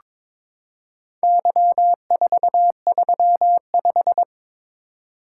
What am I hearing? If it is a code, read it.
Morse code, heard as a keyed tone, about 22 wpm, 700 Hz. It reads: Y435
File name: practice / 22wpm / Four Letter-Number Combinations - 22wpm